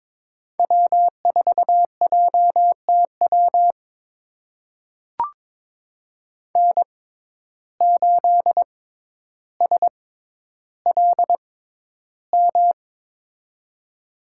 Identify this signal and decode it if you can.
Morse code, heard as a keyed tone, about 22 wpm, spaced as for 11 wpm, 700 Hz. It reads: W4JTW N8SLM
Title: USA Call Signs - 5 Characters - 6x Character Spacing - 22wpm